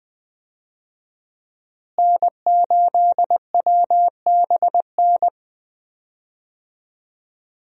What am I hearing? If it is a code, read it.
Morse code, heard as a keyed tone, about 20 wpm, 700 Hz. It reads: N8WBN
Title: USA Call Signs - 5 Characters - Repeated 2x - 20wpm